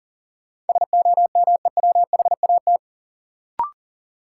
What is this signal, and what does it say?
Morse code, heard as a keyed tone, about 40 wpm, 700 Hz. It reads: SOMEWHAT
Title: Sentences from Top 1000 Words - Review Part 1 - New Word Drill - 40wpm